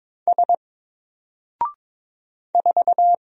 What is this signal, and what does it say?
Morse code, heard as a keyed tone, about 22 wpm, 700 Hz. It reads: S 4
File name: Single-Letter-Number-22wpm